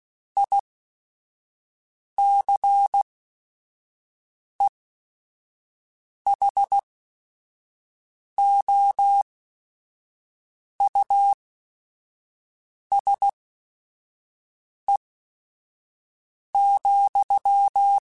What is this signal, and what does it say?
Morse code, heard as a keyed tone, about 16 wpm, spaced as for 5 wpm, 800 Hz. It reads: ICEHOUSE,
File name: General / General-mp3s / Houses @5